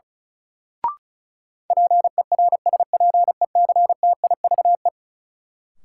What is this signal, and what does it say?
Morse code, heard as a keyed tone, about 36 wpm, 700 Hz. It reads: PERSPECTIVE
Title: Sentences from Top 1200 Words - New Word Drill Alt - 35wpm